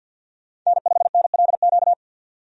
Morse code, heard as a keyed tone, about 47 wpm, 700 Hz. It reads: N5NLQ
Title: USA Call Signs - 5 Characters - 50wpm